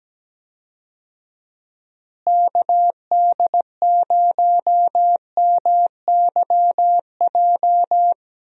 Morse code, heard as a keyed tone, about 17 wpm, 700 Hz. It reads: KD0MYJ
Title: USA Call Signs - 6 Characters - Repeated 2x - 17wpm